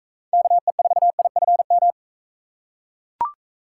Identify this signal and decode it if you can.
Morse code, heard as a keyed tone, about 42 wpm, 700 Hz. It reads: KE4IFM